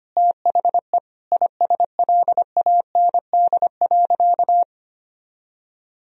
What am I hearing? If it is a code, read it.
Morse code, heard as a keyed tone, about 25 wpm, 700 Hz. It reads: THE ISLAND.
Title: Sentences from Top 1000 Words - Review Part 3 - 25wpm